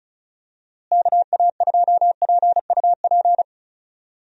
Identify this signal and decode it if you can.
Morse code, heard as a keyed tone, about 36 wpm, 700 Hz. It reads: KA2PUP
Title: USA Call Signs - 6 Characters - Repeated 3x - 35wpm